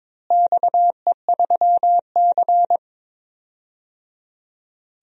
Morse code, heard as a keyed tone, about 22 wpm, 700 Hz. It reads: XE3C
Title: DXCC-100 Call Signs - 4 Characters - Repeated 3x - 22wpm